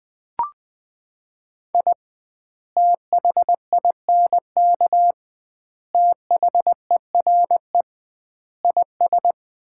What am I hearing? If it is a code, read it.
Morse code, heard as a keyed tone, about 20 wpm, 700 Hz. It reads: I THINK THERE IS
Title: 2x Word-Spacing - Sets of 5 Words - 20wpm